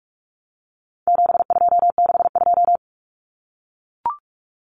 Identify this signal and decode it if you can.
Morse code, heard as a keyed tone, about 44 wpm, 700 Hz. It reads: 7262
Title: Common Cut Numbers - 4-Digit 45wpm